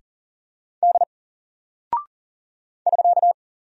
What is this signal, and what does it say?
Morse code, heard as a keyed tone, about 40 wpm, 700 Hz. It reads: D <SK>
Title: Single Character (Letter:Number:Punctuation:Prosign) Alt 40wpm